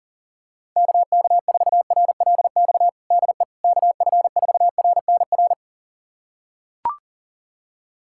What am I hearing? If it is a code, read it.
Morse code, heard as a keyed tone, about 40 wpm, 700 Hz. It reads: KK4RLX DE KF4RNR
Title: USA Callsign DE Callsign - 6 Characters - 40wpm